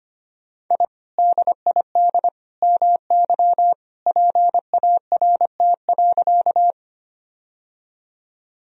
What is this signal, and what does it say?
Morse code, heard as a keyed tone, about 25 wpm, 700 Hz. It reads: I DID MY PART.
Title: Sentences from Top 1000 Words - Review Part 1 - 25wpm